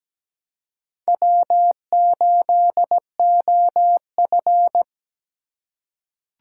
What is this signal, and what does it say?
Morse code, heard as a keyed tone, about 17 wpm, 700 Hz. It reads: W8OF
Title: USA Call Signs - 4 Characters - 2x Character Spacing - 17wpm